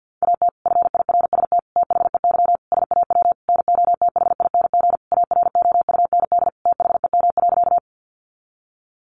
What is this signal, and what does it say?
Morse code, heard as a keyed tone, about 50 wpm, 700 Hz. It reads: AT FIRST THEY SAW NOTHING AROUND THEM.
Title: Sentences from Top 600 Words 50wpm